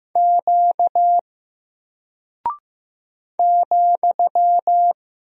Morse code, heard as a keyed tone, about 15 wpm, 700 Hz. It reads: Q ,
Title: Single Character (Letter:Number:Punctuation:Prosign) 15wpm